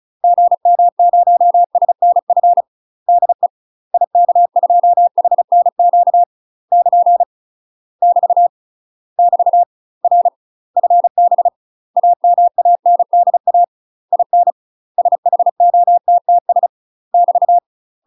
Morse code, heard as a keyed tone, about 36 wpm, 700 Hz. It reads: GM0SNF DE IK2HNQ <KN> <BT> <BT> R FB AMANDA IN SHOTTS <BT>
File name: UK MASTER QSOs TRAINING / 35WPM / UK MASTER QSOs 0013